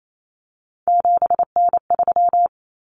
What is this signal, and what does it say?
Morse code, heard as a keyed tone, about 28 wpm, 700 Hz. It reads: 7N3
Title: Common Cut Numbers - 3-Digit 28wpm